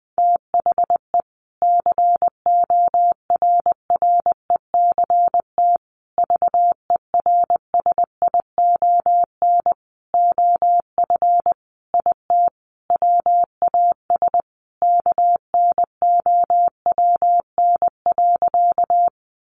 Morse code, heard as a keyed tone, about 20 wpm, 700 Hz. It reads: THE CORRECT VERSION OF IT WAS KNOWN.
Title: Sentences from Top 1400 Words 20wpm